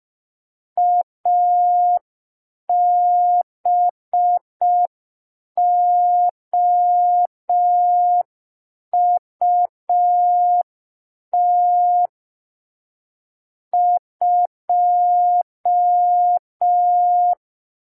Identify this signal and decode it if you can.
Morse code, heard as a keyed tone, about 5 wpm, 710 Hz. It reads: ABOUT 2
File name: g3zpd / Morse code P4